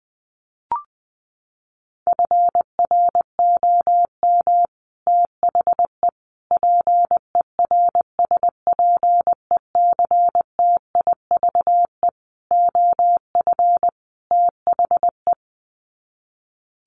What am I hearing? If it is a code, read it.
Morse code, heard as a keyed tone, about 20 wpm, 700 Hz. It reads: FROM THE PERSPECTIVE OF THE
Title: Sets of 5 Words Encore2 20wpm